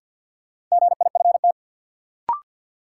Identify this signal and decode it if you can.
Morse code, heard as a keyed tone, about 52 wpm, 700 Hz. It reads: GIFT
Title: Sentences from Top 900 Words - New Word Drill - 50wpm